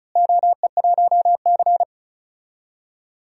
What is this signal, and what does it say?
Morse code, heard as a keyed tone, about 36 wpm, 700 Hz. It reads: OE1C
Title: DXCC-100 Call Signs - 4 Characters - Repeated 3x - 35wpm